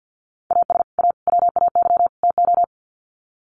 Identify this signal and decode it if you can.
Morse code, heard as a keyed tone, about 51 wpm, 700 Hz. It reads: AS A WAY TO
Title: Sets of 4 Words 50wpm